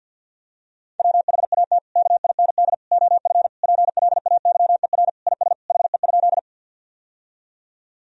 Morse code, heard as a keyed tone, about 50 wpm, 700 Hz. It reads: WHAT KIND OF PLAYER IS HE?
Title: Sentences from Top 1000 Words - Review Part 4 - 50wpm